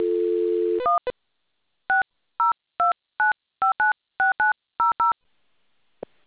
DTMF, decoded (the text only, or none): 1 6029596900